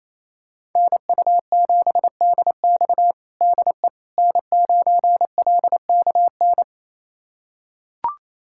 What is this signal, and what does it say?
Morse code, heard as a keyed tone, about 27 wpm, 700 Hz. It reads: NU7DX DE N9LKN
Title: USA Callsign DE Callsign - 5 Characters - 28wpm